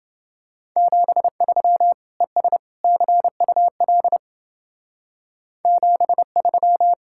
Morse code, heard as a keyed tone, about 28 wpm, 700 Hz. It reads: 73 ES CUL 73